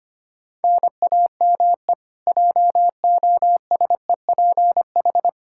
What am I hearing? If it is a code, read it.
Morse code, heard as a keyed tone, about 25 wpm, 700 Hz. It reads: NAME JOSEPH
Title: K1USN SST QSO Practice 25wpm